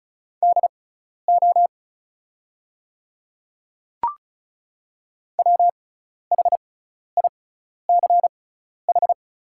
Morse code, heard as a keyed tone, about 36 wpm, spaced as for 12 wpm, 690 Hz. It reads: DO WHICH
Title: 6x Character Spacing - Top 100 Words - 35wpm